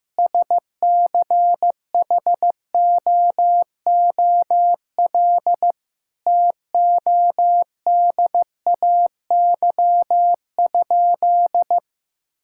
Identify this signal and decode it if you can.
Morse code, heard as a keyed tone, about 16 wpm, 700 Hz. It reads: SCHOOL TODAY?
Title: Sentences from Top 1000 Words - Review Part 5 - 15wpm